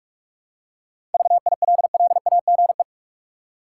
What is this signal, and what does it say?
Morse code, heard as a keyed tone, about 46 wpm, 700 Hz. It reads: VILLAGE
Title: Seven Letter Words - 45wpm